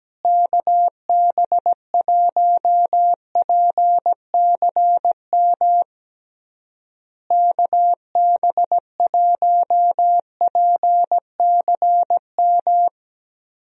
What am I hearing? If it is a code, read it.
Morse code, heard as a keyed tone, about 17 wpm, 700 Hz. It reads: KB1PCM KB1PCM
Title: USA Call Signs - 6 Characters - Repeated 3x - 17wpm